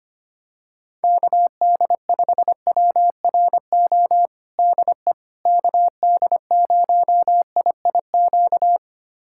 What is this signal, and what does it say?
Morse code, heard as a keyed tone, about 25 wpm, 700 Hz. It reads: KD5WRO DE KD0IIQ
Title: USA Callsign DE Callsign - 6 Characters - 25wpm